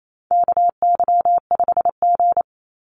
Morse code, heard as a keyed tone, about 28 wpm, 700 Hz. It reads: KY5G
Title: USA Call Signs 28wpm